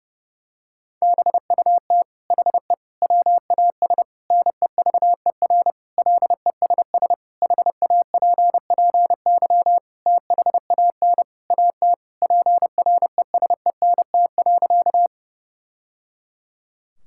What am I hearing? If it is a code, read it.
Morse code, heard as a keyed tone, about 30 wpm, 700 Hz. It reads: BUT HE WAS NEVER LESS HAPPY THAN AT PRESENT.
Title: Sentences from Top 1000 Words - Review Part 5 - 30wpm